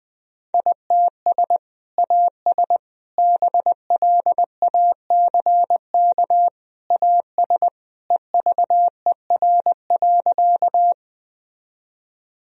Morse code, heard as a keyed tone, about 20 wpm, 700 Hz. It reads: ITS AS BLACK AS EVER.
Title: Sentences from Top 1000 Words - Review Part 2 - 20wpm